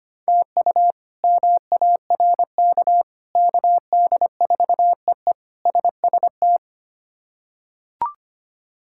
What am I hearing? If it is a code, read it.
Morse code, heard as a keyed tone, about 25 wpm, 700 Hz. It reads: TU MARK KD4EE SST
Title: K1USN SST QSO Practice - 25wpm